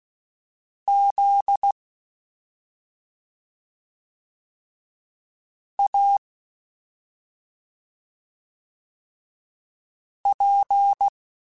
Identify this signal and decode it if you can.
Morse code, heard as a keyed tone, about 16 wpm, spaced as for 2 wpm, 800 Hz. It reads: ZAP
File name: Special INGs-mp3s / Special INGs-mp3s / Slow ing @10